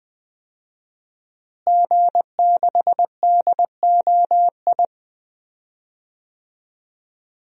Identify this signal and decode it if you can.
Morse code, heard as a keyed tone, about 20 wpm, 700 Hz. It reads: G6DOI
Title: DXCC-100 Call Signs - 5 Characters - Repeated 3x - 20wpm